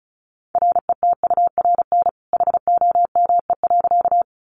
Morse code, heard as a keyed tone, about 34 wpm, 700 Hz. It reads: RETURN HOME.
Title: Sentences from Top 800 Words 35wpm